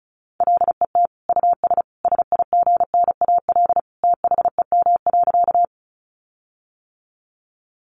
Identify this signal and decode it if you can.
Morse code, heard as a keyed tone, about 34 wpm, 700 Hz. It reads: LET US SIGNAL THEM.
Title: Sentences from Top 1400 Words 35wpm